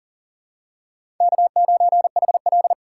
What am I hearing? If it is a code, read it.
Morse code, heard as a keyed tone, about 38 wpm, 700 Hz. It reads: K9HL
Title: USA Call Signs - 4 Characters - 4x Character Spacing - 40wpm